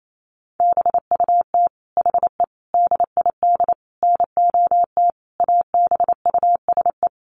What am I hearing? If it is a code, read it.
Morse code, heard as a keyed tone, about 28 wpm, 700 Hz. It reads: BUT HE DID NOT ABUSE
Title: Sentences from Top 1300 Words 28wpm